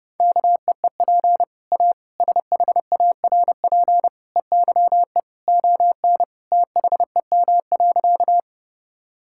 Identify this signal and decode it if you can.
Morse code, heard as a keyed tone, about 32 wpm, 700 Hz. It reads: KEEP A SHARP EYE ON THEM.
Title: Sentences from Top 1600 Words 30wpm